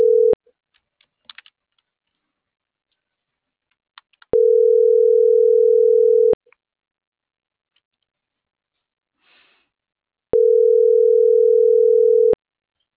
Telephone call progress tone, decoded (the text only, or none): ringback tone